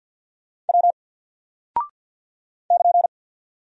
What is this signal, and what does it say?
Morse code, heard as a keyed tone, about 48 wpm, 700 Hz. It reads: W <KN>
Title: Single Character (Letter:Number:Punctuation:Prosign) 50wpm